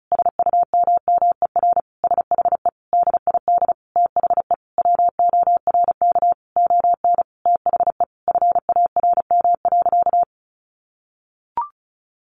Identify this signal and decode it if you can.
Morse code, heard as a keyed tone, about 34 wpm, 700 Hz. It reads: SUMMER SHE DID THE WORK ON THE FARM.
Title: Sentences from Top 1200 Words 35wpm